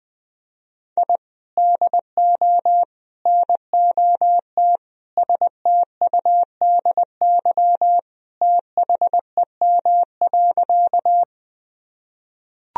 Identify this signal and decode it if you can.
Morse code, heard as a keyed tone, about 20 wpm, 700 Hz. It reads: I DO NOT STUDY THEM.
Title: Sentences from Top 1000 Words - Review Part 1 - 20wpm